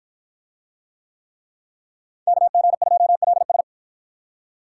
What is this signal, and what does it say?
Morse code, heard as a keyed tone, about 53 wpm, 700 Hz. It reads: KG2LS